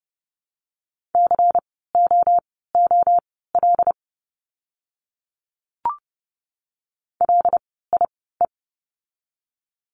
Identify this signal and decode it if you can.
Morse code, heard as a keyed tone, about 31 wpm, spaced as for 17 wpm, 700 Hz. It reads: COOL LIE
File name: Lesson 16f - Letter c - Words 30wpm